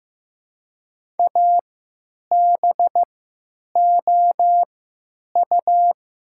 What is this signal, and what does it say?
Morse code, heard as a keyed tone, about 15 wpm, spaced as for 9 wpm, 700 Hz. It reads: ABOU
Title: 3x Character Spacing - Top 100 Words - 15wpm